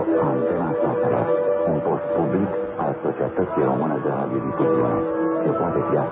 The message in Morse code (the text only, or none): none